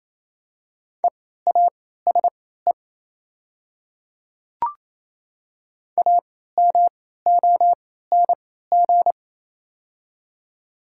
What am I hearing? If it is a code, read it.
Morse code, heard as a keyed tone, about 28 wpm, spaced as for 16 wpm, 700 Hz. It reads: EASE AMONG
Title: 3x Character Spacing - Top 500 Words - 28wpm